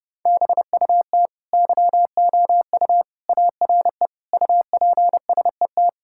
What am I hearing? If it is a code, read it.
Morse code, heard as a keyed tone, about 30 wpm, 700 Hz. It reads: BUT YOU ARE UPSET
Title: Sentences from Top 1000 Words - Review Part 5 - 30wpm